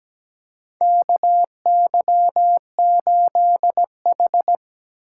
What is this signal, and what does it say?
Morse code, heard as a keyed tone, about 17 wpm, 700 Hz. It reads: KY8H
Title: USA Call Signs - 4x Character Spacing - 17wpm